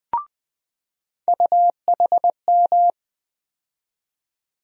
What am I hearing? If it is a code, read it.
Morse code, heard as a keyed tone, about 20 wpm, 700 Hz. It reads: UHM
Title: USA Call Sign Postfixes - 3 Letters 20wpm